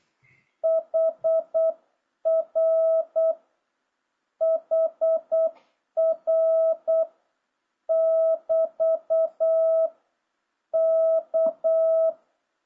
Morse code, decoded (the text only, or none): HR HR <BT> K